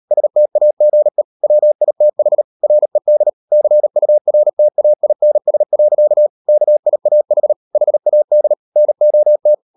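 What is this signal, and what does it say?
Morse code, heard as a keyed tone, about 38 wpm, 600 Hz. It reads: STAGE WITH RED CURTAINS. KIAH HAD NOT
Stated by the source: Morse code practice at 38 WPM, sending Ken Follett's novel Never in English, from CW med Gnister.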